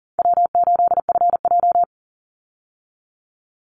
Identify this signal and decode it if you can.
Morse code, heard as a keyed tone, about 40 wpm, 700 Hz. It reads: W8FJ
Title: Easy Call Signs Repeated 2x - 40wpm